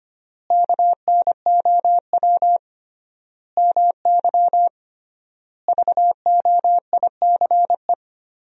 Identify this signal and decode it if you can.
Morse code, heard as a keyed tone, about 25 wpm, 700 Hz. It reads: KNOW MY VOICE
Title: 3x Word-Spacing - Sentences from Top 500 Words - 25wpm